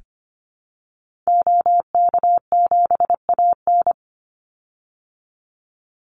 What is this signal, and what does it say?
Morse code, heard as a keyed tone, about 25 wpm, 700 Hz. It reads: OK7AN
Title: Call Signs Repeated 2x - 25wpm